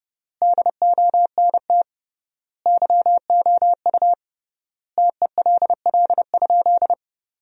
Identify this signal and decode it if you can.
Morse code, heard as a keyed tone, about 30 wpm, 700 Hz. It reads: DONT YOU TELL?
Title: 3x Word-Spacing - Sentences from Top 300 Words - 30wpm